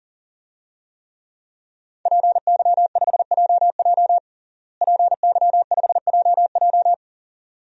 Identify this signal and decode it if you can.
Morse code, heard as a keyed tone, about 41 wpm, 700 Hz. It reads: PY5JJ PY5JJ
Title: DXCC-100 Call Signs - 5 Characters - Repeated 2x - 40wpm